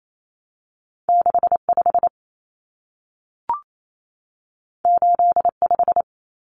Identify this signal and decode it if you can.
Morse code, heard as a keyed tone, about 28 wpm, 700 Hz. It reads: 65 85
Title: Numbers - 2-Digit 28wpm